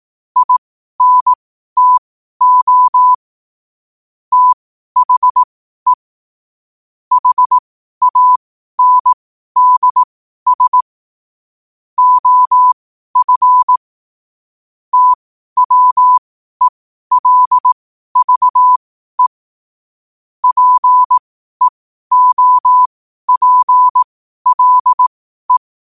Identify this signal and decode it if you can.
Morse code, heard as a keyed tone, about 18 wpm, spaced as for 12 wpm, 1000 Hz. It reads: INTO THE HANDS OF TWELVE PEOPLE
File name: qotdfiles / 12wpm